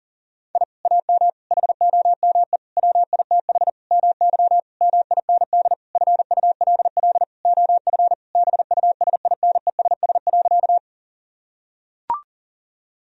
Morse code, heard as a keyed tone, about 40 wpm, 700 Hz. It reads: I AM HOME WITH MY MIND FULL OF BUSINESS.
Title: Sentences from Top 1000 Words - Review Part 3 - 40wpm